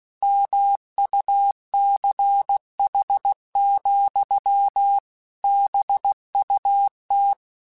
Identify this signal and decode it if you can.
Morse code, heard as a keyed tone, about 16 wpm, 800 Hz. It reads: MUCH, BUT